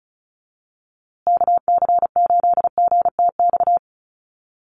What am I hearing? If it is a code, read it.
Morse code, heard as a keyed tone, about 34 wpm, 700 Hz. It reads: KC8GTX
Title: USA Call Signs Repeated 2x 35wpm